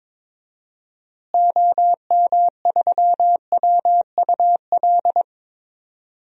Morse code, heard as a keyed tone, about 22 wpm, 700 Hz. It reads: OM3WUL
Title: DXCC-100 Call Signs - 6 Characters - Repeated 3x - 22wpm